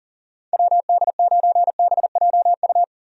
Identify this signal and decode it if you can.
Morse code, heard as a keyed tone, about 40 wpm, 700 Hz. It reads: WD9BJU
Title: USA Call Signs - 6 Characters - 40wpm